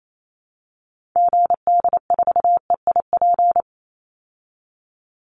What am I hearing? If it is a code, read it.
Morse code, heard as a keyed tone, about 28 wpm, 690 Hz. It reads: GD4EIP
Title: Call Signs Repeated 3x - 28wpm